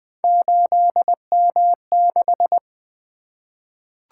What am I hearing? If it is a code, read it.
Morse code, heard as a keyed tone, about 20 wpm, 700 Hz. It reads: 8M6
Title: Three Letter-Number Combinations - 20wpm